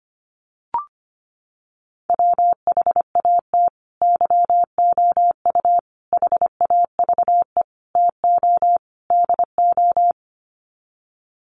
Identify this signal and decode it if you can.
Morse code, heard as a keyed tone, about 25 wpm, 700 Hz. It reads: WHAT YOU HAVE TO DO
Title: Sets of 5 Words Encore 25wpm